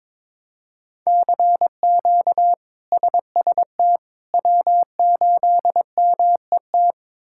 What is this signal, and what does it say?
Morse code, heard as a keyed tone, about 22 wpm, 700 Hz. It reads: CQ SST W8MET
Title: K1USN SST QSO Practice - 22wpm